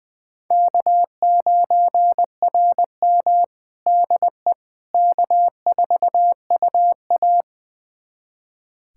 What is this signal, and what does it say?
Morse code, heard as a keyed tone, about 20 wpm, 700 Hz. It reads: K9RM DE K4UA
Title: USA Callsign DE Callsign - 4 Characters - 20wpm